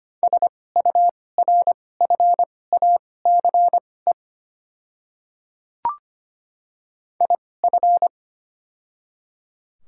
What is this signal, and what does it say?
Morse code, heard as a keyed tone, about 25 wpm, spaced as for 18 wpm, 700 Hz. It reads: SURFACE IF
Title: 2x Character Spacing - Top 500 Words - 25wpm